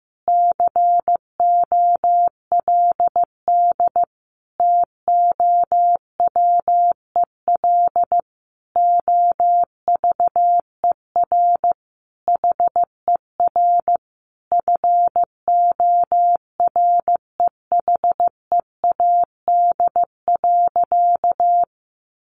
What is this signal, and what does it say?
Morse code, heard as a keyed tone, about 15 wpm, 700 Hz. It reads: COLD TOWEL OVER HER FOREHEAD.
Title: Easy Phrases 15wpm